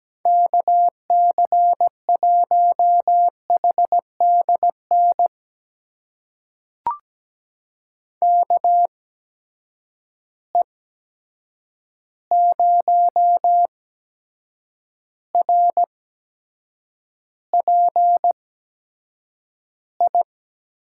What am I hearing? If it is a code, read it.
Morse code, heard as a keyed tone, about 17 wpm, spaced as for 7 wpm, 700 Hz. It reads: KC1HDN KE0RPI
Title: USA Call Signs - 8x Character Spacing - 17wpm